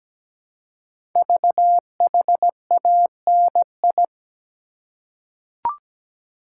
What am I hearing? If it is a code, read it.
Morse code, heard as a keyed tone, about 17 wpm, 700 Hz. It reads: VHANI
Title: Code Groups - 5 Letters - 17wpm